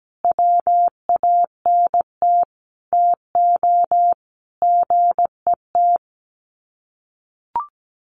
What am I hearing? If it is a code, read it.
Morse code, heard as a keyed tone, about 17 wpm, 700 Hz. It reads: WANT TO GET
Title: Sets of 5 Words Encore 17wpm